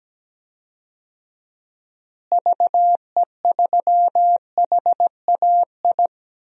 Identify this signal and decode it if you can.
Morse code, heard as a keyed tone, about 17 wpm, 700 Hz. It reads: VE3HAI
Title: DXCC-100 Call Signs - 6 Characters - Repeated 2x - 17wpm